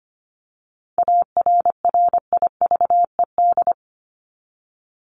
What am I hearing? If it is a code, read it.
Morse code, heard as a keyed tone, about 25 wpm, 700 Hz. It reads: ARRIVED
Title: ED Suffix - Top 100 Words with Base Words 25wpm